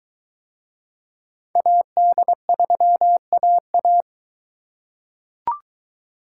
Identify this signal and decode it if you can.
Morse code, heard as a keyed tone, about 22 wpm, 700 Hz. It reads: AD3AA